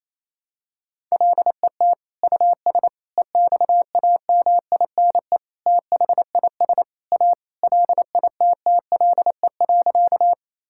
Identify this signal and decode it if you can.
Morse code, heard as a keyed tone, about 28 wpm, 700 Hz. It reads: LET US EXAMINE THIS A LITTLE.
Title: Sentences from Top 1600 Words 28wpm